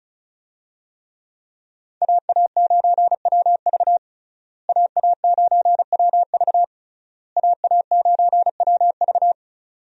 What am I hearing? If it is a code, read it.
Morse code, heard as a keyed tone, about 36 wpm, 700 Hz. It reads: AA9WV AA9WV AA9WV
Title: USA Call Signs - 5 Characters - Repeated 3x - 35wpm